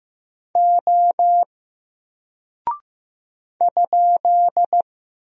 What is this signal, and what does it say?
Morse code, heard as a keyed tone, about 15 wpm, 700 Hz. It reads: O ?
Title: Single Character (Letter:Number:Punctuation:Prosign) 15wpm